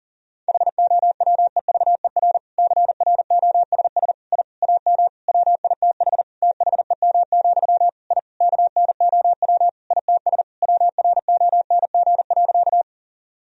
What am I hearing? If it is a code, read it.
Morse code, heard as a keyed tone, about 40 wpm, 700 Hz. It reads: HOWEVER CROSS I AM WITH THEM, I KNOW ITS WRONG.
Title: Sentences from Top 1000 Words - Review Part 1 - 40wpm